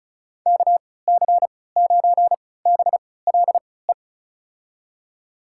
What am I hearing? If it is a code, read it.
Morse code, heard as a keyed tone, about 36 wpm, spaced as for 20 wpm, 700 Hz. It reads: KC9BLE
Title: USA Call Signs - 6 Characters - 3x Character Spacing - 35wpm